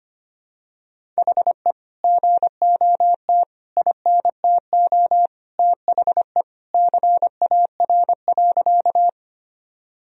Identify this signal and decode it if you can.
Morse code, heard as a keyed tone, about 25 wpm, 700 Hz. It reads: HE GOT INTO THE CAR.
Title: Sentences from Top 1000 Words - Review Part 1 - 25wpm